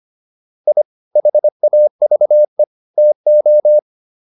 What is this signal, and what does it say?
Morse code, heard as a keyed tone, about 25 wpm, 600 Hz. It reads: I HAVE TO